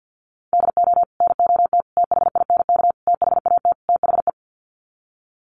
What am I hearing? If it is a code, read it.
Morse code, heard as a keyed tone, about 50 wpm, 700 Hz. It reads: DO NOT THINK THAT THE
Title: Sets of 5 Words Encore2 50wpm